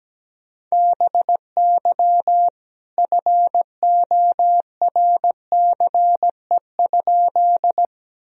Morse code, heard as a keyed tone, about 17 wpm, 700 Hz. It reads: BY FORCE?
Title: Sentences from Top 1000 Words - Review Part 3 - 17wpm